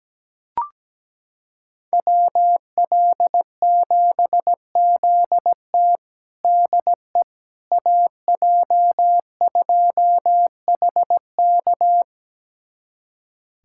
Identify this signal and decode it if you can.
Morse code, heard as a keyed tone, about 17 wpm, 700 Hz. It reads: WL7ZT DE AJ2HK
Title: USA Callsign DE Callsign - 5 Characters - 17wpm